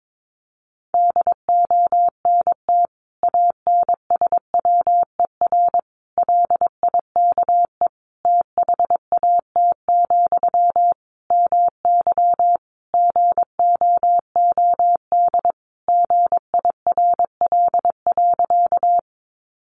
Morse code, heard as a keyed tone, about 22 wpm, 700 Hz. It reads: DONT ANSWER LIKE THAT, MY GOOD GIRL.
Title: Sentences from Top 400 Words 22wpm